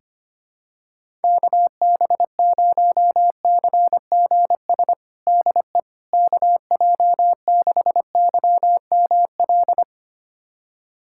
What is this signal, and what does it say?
Morse code, heard as a keyed tone, about 25 wpm, 700 Hz. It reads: KB0CGS DE KJ6YML
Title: USA Callsign DE Callsign - 6 Characters - 25wpm